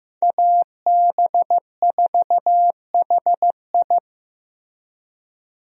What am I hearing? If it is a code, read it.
Morse code, heard as a keyed tone, about 15 wpm, 700 Hz. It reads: AB4HI
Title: USA Call Signs - 5 Characters - Repeated 3x - 15wpm